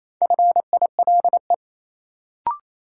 Morse code, heard as a keyed tone, about 28 wpm, 700 Hz. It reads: FILE
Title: Sentences from Top 1200 Words - New Word Drill Alt - 28wpm